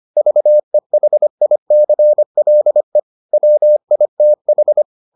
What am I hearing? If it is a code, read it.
Morse code, heard as a keyed tone, about 25 wpm, 600 Hz. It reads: VEHICLE WITH